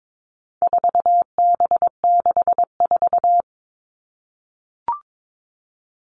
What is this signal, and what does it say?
Morse code, heard as a keyed tone, about 22 wpm, 700 Hz. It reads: 4B64
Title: All Cut Numbers - 4-Digit 22wpm